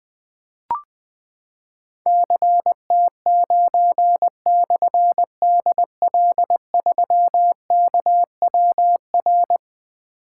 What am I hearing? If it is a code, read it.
Morse code, heard as a keyed tone, about 20 wpm, 700 Hz. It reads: CT9/DL3KWR
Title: Difficult Call Signs 20wpm